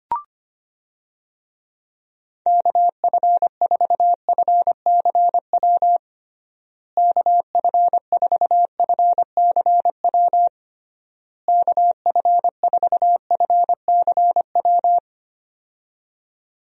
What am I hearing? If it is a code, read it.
Morse code, heard as a keyed tone, about 25 wpm, 700 Hz. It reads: KF4FCW KF4FCW KF4FCW
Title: USA Call Signs - 6 Characters - Repeated 3x - 25wpm